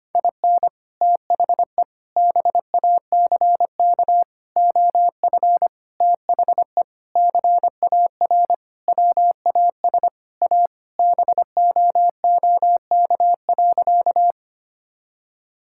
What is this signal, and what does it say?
Morse code, heard as a keyed tone, about 25 wpm, 700 Hz. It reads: IN THE BACK OF THE CAR WAS A BOOK.
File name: practice / 25wpm / Sentences with 4-Word Phrases - 25wpm